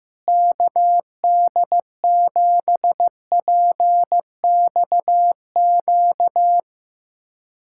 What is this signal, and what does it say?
Morse code, heard as a keyed tone, about 15 wpm, 700 Hz. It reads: KD7PXQ
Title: USA Callsign DE Callsign - 6 Characters - 15wpm